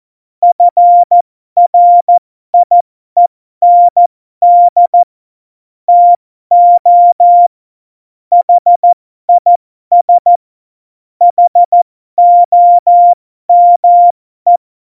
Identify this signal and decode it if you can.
Morse code, heard as a keyed tone, about 14 wpm, spaced as for 12 wpm, 700 Hz. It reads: FRIEND TO HIS HOME
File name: Morsekurs / Witze